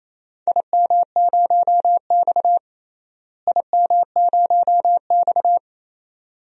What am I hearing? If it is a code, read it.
Morse code, heard as a keyed tone, about 28 wpm, 700 Hz. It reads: IM0X IM0X
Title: DXCC-100 Call Signs - 4 Characters - Repeated 2x - 28wpm